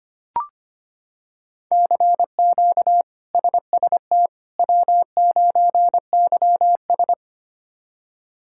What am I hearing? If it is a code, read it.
Morse code, heard as a keyed tone, about 25 wpm, 700 Hz. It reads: CQ SST W9YS
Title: K1USN SST QSO Practice - 25wpm